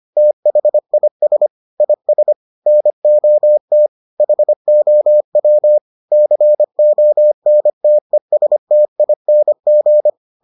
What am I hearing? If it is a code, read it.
Morse code, heard as a keyed tone, about 25 wpm, 600 Hz. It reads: THIS IS NOT HOW CONTESTING